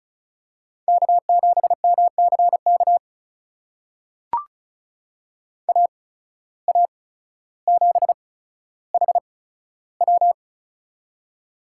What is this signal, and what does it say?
Morse code, heard as a keyed tone, about 34 wpm, spaced as for 15 wpm, 690 Hz. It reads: K7MCK AA7HW
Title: USA Call Signs - 5 Characters - 8x Character Spacing - 35wpm